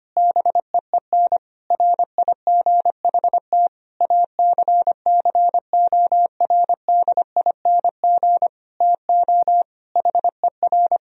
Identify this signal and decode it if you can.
Morse code, heard as a keyed tone, about 25 wpm, 700 Hz. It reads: BEEN RIGHT ACCORDING TO HER
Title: Sentences from Top 1000 Words - Review Part 4 - 25wpm